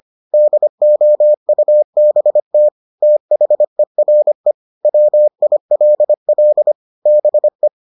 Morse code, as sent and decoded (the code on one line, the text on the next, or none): -.. --- ..- -... - / - .... .-. / .-- .. .-.. .-.. / -... .
DOUBT THERE WILL BE